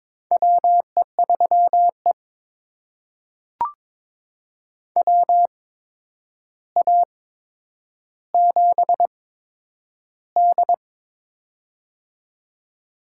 Morse code, heard as a keyed tone, about 22 wpm, spaced as for 9 wpm, 700 Hz. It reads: WE3E WA7D
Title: USA Call Signs - 4 Characters - 8x Character Spacing - 22wpm